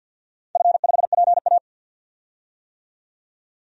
Morse code, heard as a keyed tone, about 48 wpm, 700 Hz. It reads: F5PA
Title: DXCC-100 Call Signs - 4 Characters - 50wpm